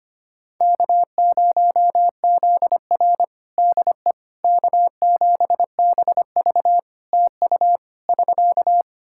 Morse code, heard as a keyed tone, about 25 wpm, 700 Hz. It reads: K0ZR DE K7BV TU <SK>
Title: Practice QSO - Exchange 4 - Ending - 25wpm